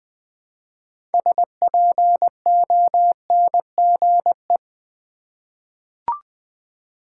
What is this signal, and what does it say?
Morse code, heard as a keyed tone, about 20 wpm, 700 Hz. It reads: SPONGE
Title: Easy Words with Hints - 20wpm